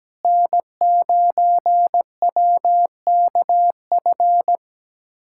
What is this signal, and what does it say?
Morse code, heard as a keyed tone, about 17 wpm, 700 Hz. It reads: N9WKF
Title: USA Callsign DE Callsign - 5 Characters - 17wpm